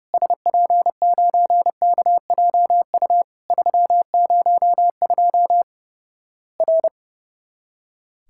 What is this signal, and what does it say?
Morse code, heard as a keyed tone, about 30 wpm, 700 Hz. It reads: SP9KJU 302 R
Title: Worked All Europe DX Contest - QTC Exchange - 30wpm